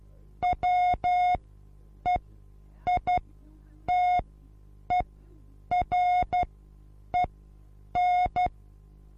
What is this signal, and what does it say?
Morse code, heard as a keyed tone, about 12 wpm, spaced as for 8 wpm, 720 Hz. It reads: WEITEREN
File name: Onlinelehrgang / morsen / DJ4UF Morsen23